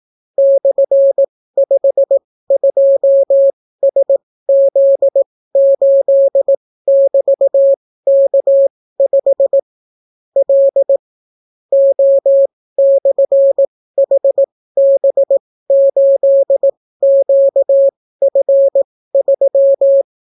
Morse code, heard as a keyed tone, about 18 wpm, spaced as for 14 wpm, 550 Hz. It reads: /52SZ8<BT>K5 L O/HB8QF3